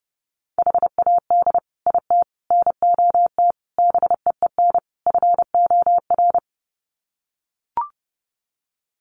Morse code, heard as a keyed tone, about 30 wpm, 700 Hz. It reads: HAD IT NOT BEEN FOR
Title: Sets of 5 Words Encore2 30wpm